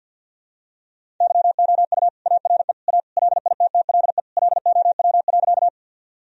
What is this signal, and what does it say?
Morse code, heard as a keyed tone, about 51 wpm, 700 Hz. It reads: YOU ARE A LITTLE LOW.
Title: Sentences from Top 1000 Words - Review Part 1 - 50wpm